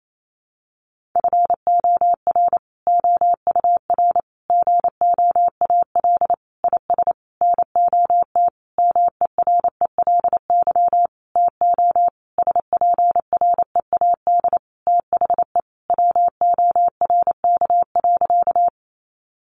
Morse code, heard as a keyed tone, about 28 wpm, 700 Hz. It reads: FOR OUR GOAL IS NOT MERELY TO SPREAD THE WORK.
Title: Sentences from Top 1400 Words 28wpm